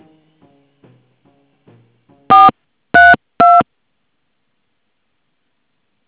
DTMF, heard as telephone touch tones, tones of about 200 ms. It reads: 432
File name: things / bcm50 fxo tellme